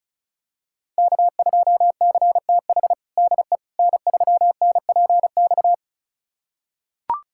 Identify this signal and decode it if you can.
Morse code, heard as a keyed tone, about 36 wpm, 700 Hz. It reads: K2CTH DE N3NPX
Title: USA Callsign DE Callsign - 5 Characters - 35wpm